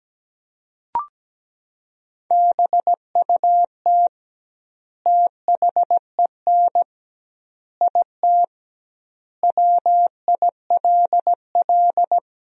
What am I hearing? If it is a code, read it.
Morse code, heard as a keyed tone, about 17 wpm, 700 Hz. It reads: BUT THEN IT WILL
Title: 2x Word-Spacing - Sentences from Top 300 Words - 17wpm